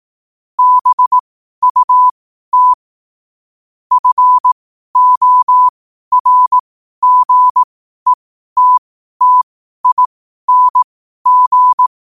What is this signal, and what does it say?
Morse code, heard as a keyed tone, about 18 wpm, spaced as for 12 wpm, 1000 Hz. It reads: BUT FORGETTING
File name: qotdfiles / 12wpm